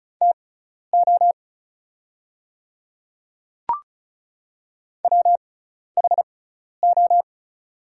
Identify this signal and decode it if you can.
Morse code, heard as a keyed tone, about 36 wpm, spaced as for 12 wpm, 700 Hz. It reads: TO WHO
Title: 6x Character Spacing - Top 300 Words - 35wpm